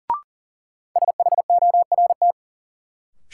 Morse code, heard as a keyed tone, about 40 wpm, 700 Hz. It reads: SHORT
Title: Sentences from Top 400 Words - New Word Drill - 40wpm